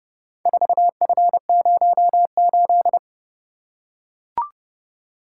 Morse code, heard as a keyed tone, about 31 wpm, 700 Hz. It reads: 4F08